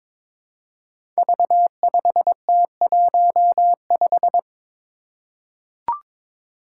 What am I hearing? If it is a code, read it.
Morse code, heard as a keyed tone, about 23 wpm, 700 Hz. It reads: V5T15
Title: Five Challenging Letter-Number Combinations 22wpm